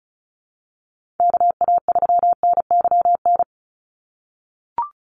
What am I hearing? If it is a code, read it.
Morse code, heard as a keyed tone, about 34 wpm, 700 Hz. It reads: KA3NYN
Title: USA Call Signs 35wpm